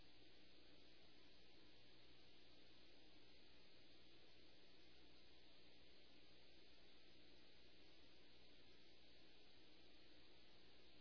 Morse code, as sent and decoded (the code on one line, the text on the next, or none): none